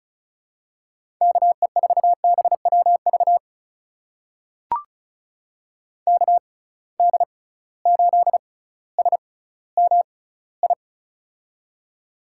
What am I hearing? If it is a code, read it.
Morse code, heard as a keyed tone, about 34 wpm, spaced as for 18 wpm, 700 Hz. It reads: KE4BWV KD8SMI
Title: USA Call Signs - 6 Characters - 6x Character Spacing - 35wpm